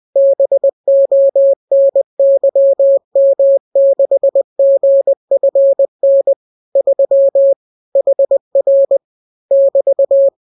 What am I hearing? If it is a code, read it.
Morse code, heard as a keyed tone, about 20 wpm, 550 Hz. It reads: BONYM6GFN 3 HR <BT>